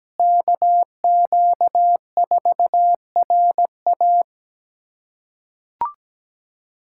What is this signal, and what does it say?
Morse code, heard as a keyed tone, about 17 wpm, 700 Hz. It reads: KQ4RA
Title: USA Callsign DE Callsign - 5 Characters - 17wpm